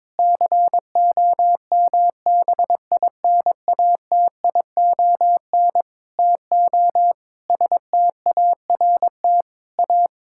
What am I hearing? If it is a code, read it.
Morse code, heard as a keyed tone, about 22 wpm, 700 Hz. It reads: COMBINATION TO START A
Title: Sentences from Top 1000 Words - Review Part 6 - 22wpm